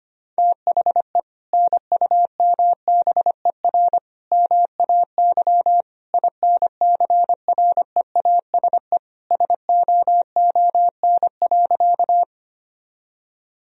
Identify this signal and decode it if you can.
Morse code, heard as a keyed tone, about 25 wpm, 700 Hz. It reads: THE NUMBER MAY INCREASE SOON.
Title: Sentences from Top 1000 Words - Review Part 4 - 25wpm